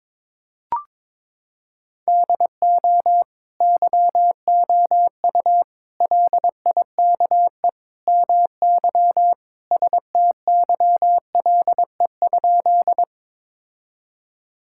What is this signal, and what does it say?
Morse code, heard as a keyed tone, about 22 wpm, 700 Hz. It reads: DO YOU LIKE MY STYLE?
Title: Sentences from Top 1000 Words - Review Part 5 - 22wpm